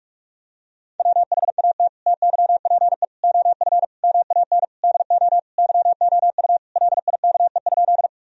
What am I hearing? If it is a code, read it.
Morse code, heard as a keyed tone, about 46 wpm, 700 Hz. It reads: WHAT TYPE OF MAN DO YOU LIKE?
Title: Sentences from Top 1000 Words - Review Part 3 - 45wpm